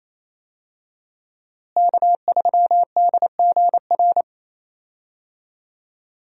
Morse code, heard as a keyed tone, about 28 wpm, 700 Hz. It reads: K3DGR
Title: USA Call Signs - 5 Characters - Repeated 3x - 28wpm